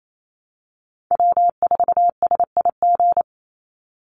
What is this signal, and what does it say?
Morse code, heard as a keyed tone, about 28 wpm, 700 Hz. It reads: W4SIG
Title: Call Signs - 28wpm